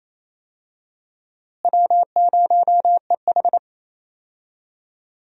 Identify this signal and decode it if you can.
Morse code, heard as a keyed tone, about 28 wpm, 700 Hz. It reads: W0EH
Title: USA Call Signs - 4 Characters - Repeated 3x - 28wpm